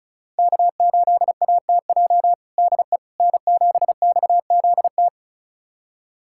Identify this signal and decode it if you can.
Morse code, heard as a keyed tone, about 34 wpm, 700 Hz. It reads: K8ATJ DE N7XZT
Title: USA Callsign DE Callsign - 5 Characters - 35wpm